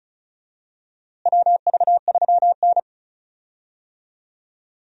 Morse code, heard as a keyed tone, about 34 wpm, 700 Hz. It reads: WV3N
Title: USA Call Signs - 4 Characters - Repeated 3x - 35wpm